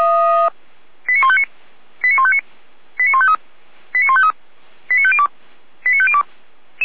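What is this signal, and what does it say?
Five-tone selective call (ZVEI), heard as quick runs of five tones, tones of about 70 ms. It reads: 89159 89159 89153 89153 89502 89502
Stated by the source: ZVEI five-tone selective call received by SDR, from the audio server of Feuerwehr Orlamünde (fire brigade).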